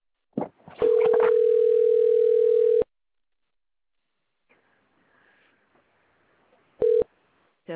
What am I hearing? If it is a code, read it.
Ringback tone (North America), 440 + 480 Hz.